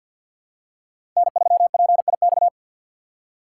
Morse code, heard as a keyed tone, about 50 wpm, 700 Hz. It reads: N3PIX
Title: USA Call Signs - 5 Characters - 3x Character Spacing - 50wpm